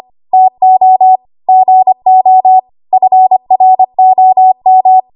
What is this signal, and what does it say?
Morse code, heard as a keyed tone, about 25 wpm, 750 Hz. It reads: TO GO FROM
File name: Archive / 25 WPM / 25WPM